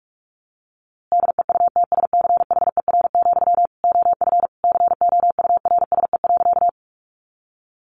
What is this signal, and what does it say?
Morse code, heard as a keyed tone, about 44 wpm, 700 Hz. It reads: DEUTSCHER, OF COURSE.
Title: A Sound Of Thunder 45wpm